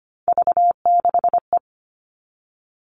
Morse code, heard as a keyed tone, about 25 wpm, 700 Hz. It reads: V6E
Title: All Cut Numbers - 3-Digit 25wpm